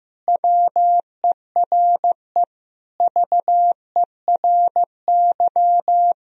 Morse code, heard as a keyed tone, about 15 wpm, 700 Hz. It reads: WERE VERY
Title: Sentences from Top 1000 Words - Review Part 2 - 15wpm